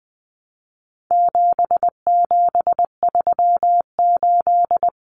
Morse code, heard as a keyed tone, about 20 wpm, 700 Hz. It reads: 7738
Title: Numbers - 4-Digit 20wpm